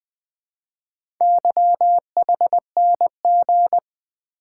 Morse code, heard as a keyed tone, about 20 wpm, 700 Hz. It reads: YHNG
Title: Four Letter-Number Combinations - 20wpm